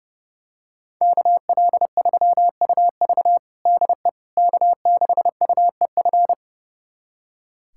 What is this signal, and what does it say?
Morse code, heard as a keyed tone, about 32 wpm, 700 Hz. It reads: KL3UV DE K6UEF
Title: USA Callsign DE Callsign - 5 Characters - 30wpm